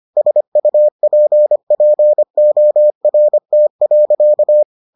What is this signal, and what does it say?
Morse code, heard as a keyed tone, about 25 wpm, 600 Hz. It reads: SUPPORT.